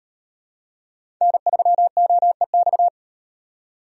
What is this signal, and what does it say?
Morse code, heard as a keyed tone, about 38 wpm, 700 Hz. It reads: N3OEX